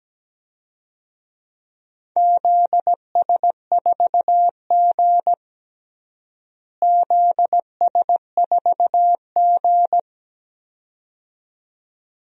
Morse code, heard as a keyed tone, about 17 wpm, 700 Hz. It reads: ZS4G ZS4G
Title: DXCC-100 Call Signs - 4 Characters - Repeated 2x - 17wpm